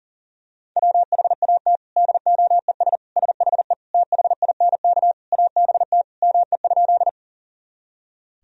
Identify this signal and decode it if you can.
Morse code, heard as a keyed tone, about 40 wpm, 700 Hz. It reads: WHAT DOES SHE THINK ABT ME?
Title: Sentences from Top 200 Words and Abbreviations 40wpm